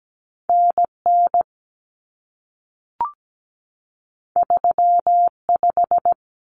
Morse code, heard as a keyed tone, about 17 wpm, 700 Hz. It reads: NN 35
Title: Common Cut Numbers - 2-Digit 17wpm